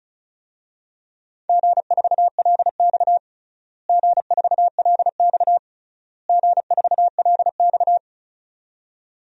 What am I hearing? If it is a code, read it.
Morse code, heard as a keyed tone, about 36 wpm, 700 Hz. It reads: G4LX G4LX G4LX
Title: DXCC-100 Call Signs - 4 Characters - Repeated 3x - 35wpm